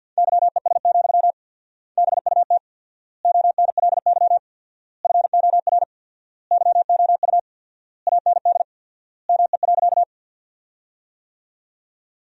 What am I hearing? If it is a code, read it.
Morse code, heard as a keyed tone, about 53 wpm, 700 Hz. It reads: YES, BUT ONLY FOR YOU AND ME.
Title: 4x Word-Spacing - Sentences from Top 500 Words - 50wpm